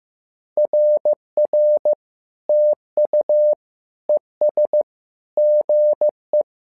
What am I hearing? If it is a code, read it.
Morse code, heard as a keyed tone, about 15 wpm, 600 Hz. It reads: RR TU ES GE
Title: SOTA QSO Practice 15wpm